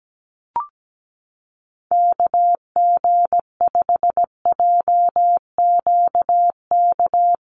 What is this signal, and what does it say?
Morse code, heard as a keyed tone, about 17 wpm, 700 Hz. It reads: KG5JQK
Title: USA Call Signs Repeated 2x 17wpm